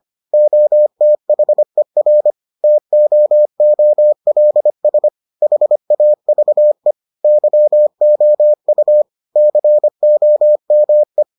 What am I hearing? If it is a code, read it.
Morse code, heard as a keyed tone, about 25 wpm, 600 Hz. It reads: OTHER TOOLS HAVE YOU COME